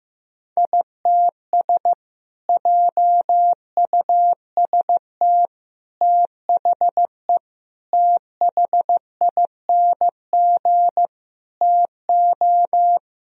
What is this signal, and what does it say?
Morse code, heard as a keyed tone, about 16 wpm, 700 Hz. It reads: ITS JUST THE THING TO
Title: Sentences from Top 1000 Words - Review Part 2 - 15wpm